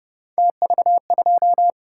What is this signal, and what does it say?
Morse code, heard as a keyed tone, about 29 wpm, 700 Hz. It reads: TV2